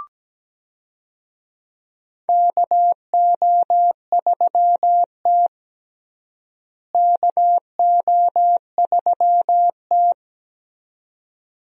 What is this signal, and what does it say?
Morse code, heard as a keyed tone, about 17 wpm, 700 Hz. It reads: KO3T KO3T
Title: USA Call Signs - 4 Characters - Repeated 2x - 17wpm